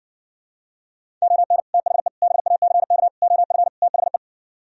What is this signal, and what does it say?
Morse code, heard as a keyed tone, about 60 wpm, 700 Hz. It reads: ON THE BACK OF THE